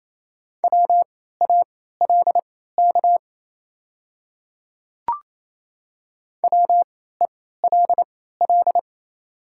Morse code, heard as a keyed tone, about 28 wpm, spaced as for 16 wpm, 700 Hz. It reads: WALK WELL
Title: 3x Character Spacing - Top 300 Words - 28wpm